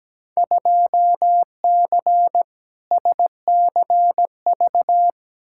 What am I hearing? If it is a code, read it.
Morse code, heard as a keyed tone, about 17 wpm, 700 Hz. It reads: 2C SCV
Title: Field Day Exchanges Encore - 17wpm